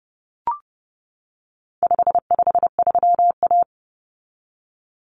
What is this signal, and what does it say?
Morse code, heard as a keyed tone, about 30 wpm, 700 Hz. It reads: 553A
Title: Common Cut Numbers - 4-Digit 30wpm